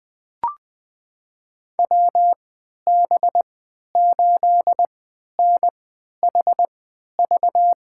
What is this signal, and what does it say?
Morse code, heard as a keyed tone, about 20 wpm, spaced as for 11 wpm, 700 Hz. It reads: WB8NHV
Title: USA Call Signs - 3x Character Spacing - 20wpm